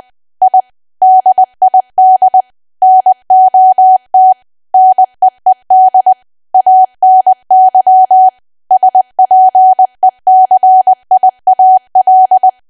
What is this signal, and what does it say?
Morse code, heard as a keyed tone, about 20 wpm, 750 Hz. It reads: I DID NOT NEED ANY SPECIAL
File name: cwa / W1AW-Level3 / 20WPM